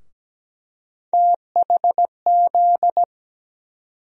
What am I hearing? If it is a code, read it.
Morse code, heard as a keyed tone, about 17 wpm, 700 Hz. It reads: THZ
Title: USA Call Sign Postfixes - 3 Letters 17wpm